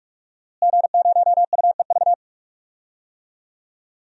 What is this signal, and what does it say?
Morse code, heard as a keyed tone, about 43 wpm, 700 Hz. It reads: G0UEV
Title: DXCC-100 Call Signs - 5 Characters - Repeated 3x - 45wpm